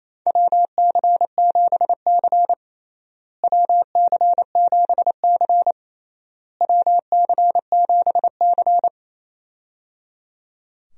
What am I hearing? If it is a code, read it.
Morse code, heard as a keyed tone, about 28 wpm, 700 Hz. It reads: WC7C WC7C WC7C
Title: USA Call Signs - 4 Characters - Repeated 3x - 28wpm